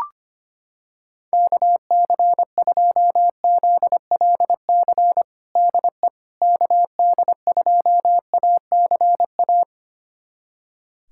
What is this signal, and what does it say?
Morse code, heard as a keyed tone, about 25 wpm, 700 Hz. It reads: KC2ZLC DE KD2ACA
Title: USA Callsign DE Callsign 25wpm